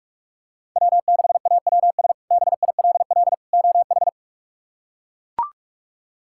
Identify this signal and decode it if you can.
Morse code, heard as a keyed tone, about 41 wpm, 700 Hz. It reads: W6AWS BILL OH
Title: K1USN SST QSO Practice - 45wpm